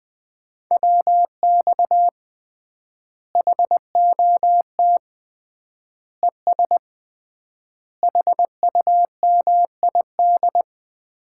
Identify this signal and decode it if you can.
Morse code, heard as a keyed tone, about 19 wpm, 700 Hz. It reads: WX HOT ES HUMID
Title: Common QSO Phrases - 3x Word-Spacing - 20wpm